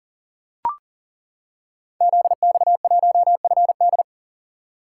Morse code, heard as a keyed tone, about 38 wpm, 700 Hz. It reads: ZX1FD
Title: Code Groups - 5 Letter:Numbers - 40wpm